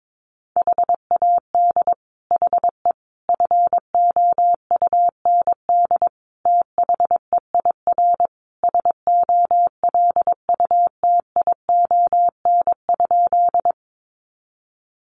Morse code, heard as a keyed tone, about 22 wpm, 700 Hz. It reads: HAD HE FOUND THEIR SOLUTION?
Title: Sentences from Top 1300 Words 22wpm